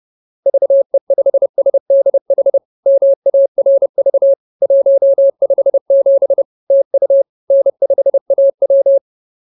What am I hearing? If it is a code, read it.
Morse code, heard as a keyed tone, about 30 wpm, 550 Hz. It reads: VE5SDH MARV 157 TU N5AW
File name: CWT-215-30